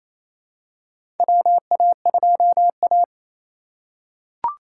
Morse code, heard as a keyed tone, about 28 wpm, 700 Hz. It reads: WA2A
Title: USA Call Signs - 3x Character Spacing - 28wpm